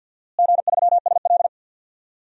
Morse code, heard as a keyed tone, about 52 wpm, 700 Hz. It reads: G3SL
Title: DXCC-100 Call Signs - 4 Characters - Repeated 2x - 50wpm